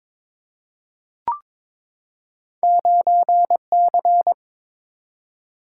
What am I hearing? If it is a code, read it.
Morse code, heard as a keyed tone, about 22 wpm, 700 Hz. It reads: 9C
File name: practice / 22wpm / Two Letter-Number Combinations - 22wpm